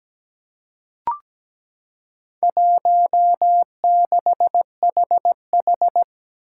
Morse code, heard as a keyed tone, about 17 wpm, 700 Hz. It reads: 16HH
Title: Four Challenging Letter-Number Combinations 17wpm